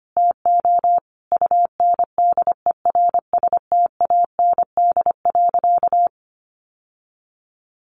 Morse code, heard as a keyed tone, about 25 wpm, 700 Hz. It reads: TO UNDERSTAND.
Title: Sentences from Top 600 Words 25wpm